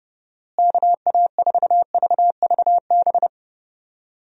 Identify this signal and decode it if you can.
Morse code, heard as a keyed tone, about 30 wpm, 700 Hz. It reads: KA4VVB
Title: USA Call Signs - 6 Characters - Repeated 2x - 30wpm